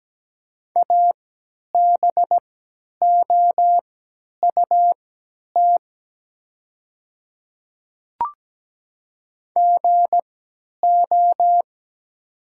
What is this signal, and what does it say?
Morse code, heard as a keyed tone, about 17 wpm, spaced as for 10 wpm, 700 Hz. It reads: ABOUT GO
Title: 3x Character Spacing - Top 100 Words - 17wpm